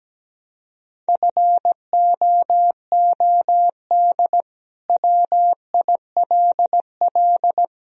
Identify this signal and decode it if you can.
Morse code, heard as a keyed tone, about 17 wpm, 700 Hz. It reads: FOOD WILL
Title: Sentences from Top 1000 Words - Review Part 2 - 17wpm